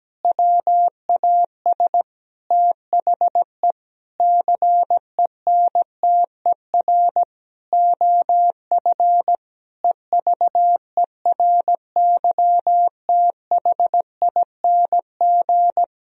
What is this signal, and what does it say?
Morse code, heard as a keyed tone, about 17 wpm, 700 Hz. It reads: WAS THE CENTER OF EVERYTHING
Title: Sentences from Top 1000 Words - Review Part 2 - 17wpm